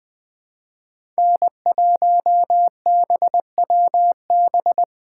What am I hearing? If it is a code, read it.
Morse code, heard as a keyed tone, about 20 wpm, 700 Hz. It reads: N1BWB
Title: USA Call Signs - 5 Characters - 3x Character Spacing - 20wpm